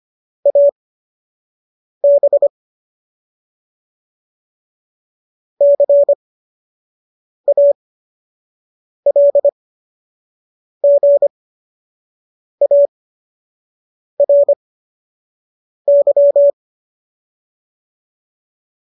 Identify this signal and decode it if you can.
Morse code, heard as a keyed tone, about 25 wpm, spaced as for 6 wpm, 570 Hz. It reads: AB CALGARY
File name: qso002 06